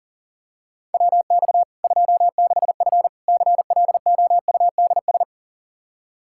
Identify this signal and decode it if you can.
Morse code, heard as a keyed tone, about 40 wpm, 700 Hz. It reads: WX 26F CLOUDS
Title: Practice QSO - Exchange 2 - Info - 40wpm